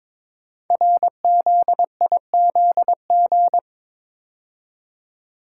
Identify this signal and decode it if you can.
Morse code, heard as a keyed tone, about 23 wpm, 700 Hz. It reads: RZIZG
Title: Code Groups - 5 Letters - 22wpm